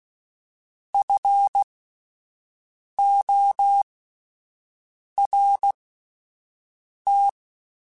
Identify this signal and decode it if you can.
Morse code, heard as a keyed tone, about 16 wpm, spaced as for 5 wpm, 800 Hz. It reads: FORT
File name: Alphabetical / Alphabetical mp3s / Fs @5